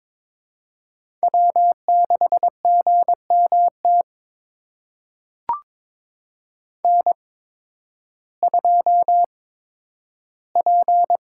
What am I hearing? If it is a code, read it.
Morse code, heard as a keyed tone, about 22 wpm, spaced as for 12 wpm, 700 Hz. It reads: W6GMT N2P
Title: USA Call Signs - 5 Characters - 8x Character Spacing - 22wpm